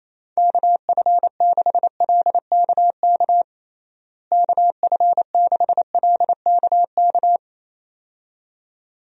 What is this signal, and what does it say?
Morse code, heard as a keyed tone, about 28 wpm, 700 Hz. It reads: KF6LKK KF6LKK
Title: USA Call Signs - 6 Characters - Repeated 2x - 28wpm